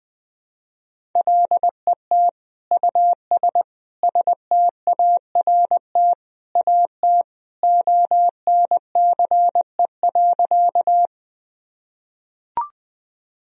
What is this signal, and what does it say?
Morse code, heard as a keyed tone, about 21 wpm, 700 Hz. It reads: LET US START AT ONCE.
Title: Sentences from Top 1000 Words - Review Part 1 - 20wpm